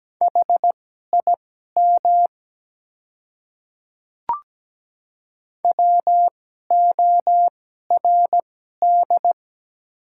Morse code, heard as a keyed tone, about 17 wpm, spaced as for 12 wpm, 700 Hz. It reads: HIM WORD
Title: 2x Character Spacing - Top 100 Words - 17wpm